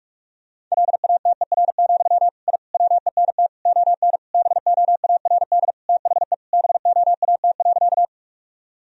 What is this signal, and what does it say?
Morse code, heard as a keyed tone, about 46 wpm, 700 Hz. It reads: LATER, I WENT ON BOARD THE BOAT.
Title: Sentences from Top 1000 Words - Review Part 4 - 45wpm